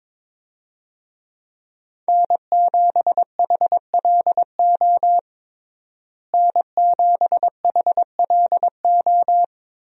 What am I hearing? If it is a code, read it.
Morse code, heard as a keyed tone, about 22 wpm, 700 Hz. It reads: N7HLO N7HLO
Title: USA Call Signs - 5 Characters - Repeated 2x - 22wpm